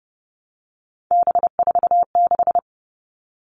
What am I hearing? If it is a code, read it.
Morse code, heard as a keyed tone, about 30 wpm, 700 Hz. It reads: B46
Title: All Cut Numbers - 3-Digit 30wpm